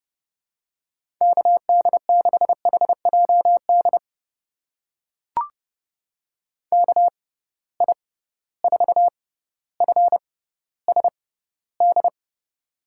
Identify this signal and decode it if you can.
Morse code, heard as a keyed tone, about 30 wpm, spaced as for 15 wpm, 700 Hz. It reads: KD6HJD KI4FSD
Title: USA Call Signs - 6 Characters - 6x Character Spacing - 30wpm